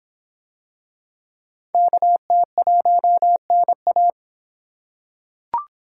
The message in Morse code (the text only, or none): KT1NA